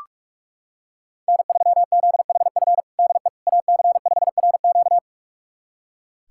Morse code, heard as a keyed tone, about 44 wpm, 700 Hz. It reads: N3ZHF DE AC5RQ
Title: USA Callsign DE Callsign - 5 Characters - 45wpm